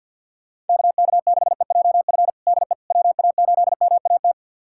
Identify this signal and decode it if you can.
Morse code, heard as a keyed tone, about 51 wpm, 700 Hz. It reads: KK6EJF DE WA8MAT
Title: USA Callsign DE Callsign 50wpm